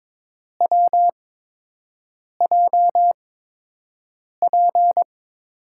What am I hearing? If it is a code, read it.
Morse code, heard as a keyed tone, about 22 wpm, spaced as for 6 wpm, 700 Hz. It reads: WJP
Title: USA Call Signs - 6 Characters - 8x Character Spacing - 22wpm